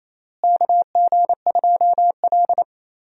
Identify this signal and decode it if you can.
Morse code, heard as a keyed tone, about 28 wpm, 700 Hz. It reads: KG2L
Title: USA Call Signs - 4 Characters - Repeated 2x - 28wpm